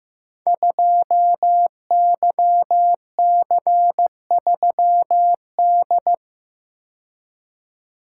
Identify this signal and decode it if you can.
Morse code, heard as a keyed tone, about 15 wpm, 700 Hz. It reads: 2YC3D
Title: Code Groups - 5 Letter:Numbers - 15wpm